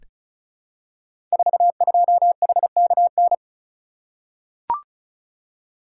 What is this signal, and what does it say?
Morse code, heard as a keyed tone, about 36 wpm, 700 Hz. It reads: 42HKN
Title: Code Groups - 5 Letter:Numbers - 35wpm